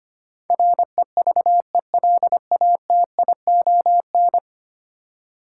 Morse code, heard as a keyed tone, about 25 wpm, 700 Hz. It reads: REVELATION
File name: 10 Letter Words - First 3 Letters Given - 25wpm